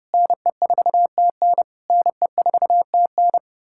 Morse code, heard as a keyed tone, about 33 wpm, 700 Hz. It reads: NE4TN NE4TN